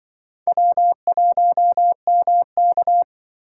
Morse code, heard as a keyed tone, about 25 wpm, 700 Hz. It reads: W1MK
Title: Speed Racing - USA Call Signs 20wpm